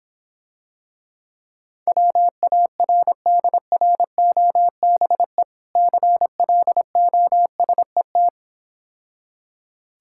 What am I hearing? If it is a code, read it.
Morse code, heard as a keyed tone, about 26 wpm, 700 Hz. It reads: WARDROBE CLOSET